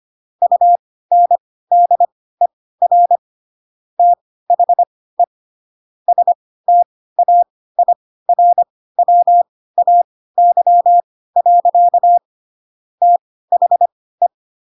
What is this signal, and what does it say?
Morse code, heard as a keyed tone, about 25 wpm, spaced as for 16 wpm, 700 Hz. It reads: UNDER THE STAIRWAY. THE
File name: Pinocchio / Pinocchio ch3 10 25 ik1hge